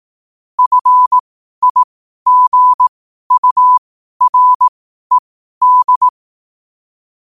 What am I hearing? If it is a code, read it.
Morse code, heard as a keyed tone, about 18 wpm, spaced as for 12 wpm, 1000 Hz. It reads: FIGURED
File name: qotdfiles / 12wpm